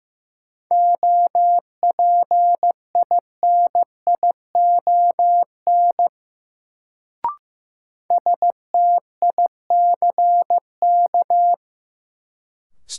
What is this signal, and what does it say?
Morse code, heard as a keyed tone, about 15 wpm, 700 Hz. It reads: OPINION STICK
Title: Sentences from Top 1200 Words - New Word Drill - 15wpm